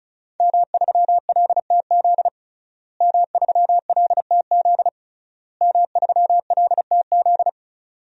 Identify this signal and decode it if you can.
Morse code, heard as a keyed tone, about 36 wpm, 700 Hz. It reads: M3LTZ M3LTZ M3LTZ
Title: DXCC-100 Call Signs - 5 Characters - Repeated 3x - 35wpm